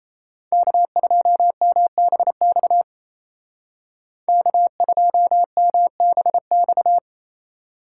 Morse code, heard as a keyed tone, about 34 wpm, 700 Hz. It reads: K2MBX K2MBX
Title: Speed Racing - USA 1x3 Call Signs 28wpm